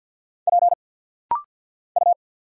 Morse code, heard as a keyed tone, about 49 wpm, 700 Hz. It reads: P U
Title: Single Letter-Number Alt - Rapid-Fire 50wpm